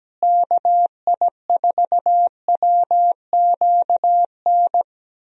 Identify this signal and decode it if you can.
Morse code, heard as a keyed tone, about 17 wpm, 700 Hz. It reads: KI4WQN
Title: USA Call Signs - 6 Characters - Repeated 2x - 17wpm